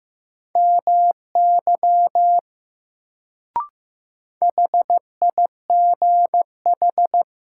Morse code, heard as a keyed tone, about 15 wpm, 700 Hz. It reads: MY HIGH
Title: Sets of 2 Words - New Word Drill - 15wpm